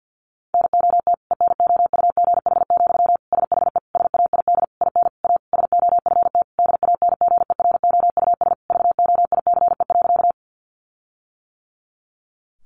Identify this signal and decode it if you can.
Morse code, heard as a keyed tone, about 50 wpm, 700 Hz. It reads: NOT ENOUGH, SHE SAID IN A SOFT DANGEROUS VOICE.
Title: Sentences from Top 1500 Words 50wpm